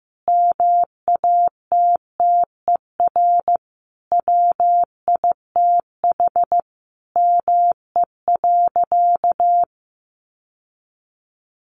Morse code, heard as a keyed tone, about 15 wpm, 700 Hz. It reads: MATTER WITH ME.
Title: Sentences from Top 700 Words 15wpm